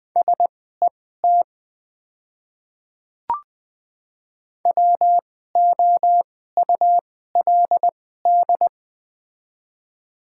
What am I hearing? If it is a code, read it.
Morse code, heard as a keyed tone, about 20 wpm, spaced as for 14 wpm, 700 Hz. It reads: SET WOULD
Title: 2x Character Spacing - Top 300 Words - 20wpm